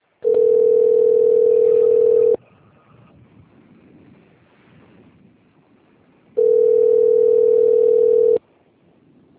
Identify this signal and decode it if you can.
Ringback tone (North America), 440 + 480 Hz.